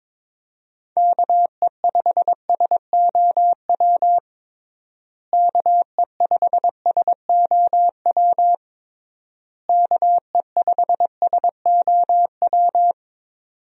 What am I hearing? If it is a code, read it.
Morse code, heard as a keyed tone, about 22 wpm, 700 Hz. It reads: KE5SOW KE5SOW KE5SOW
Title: USA Call Signs - 6 Characters - Repeated 3x - 22wpm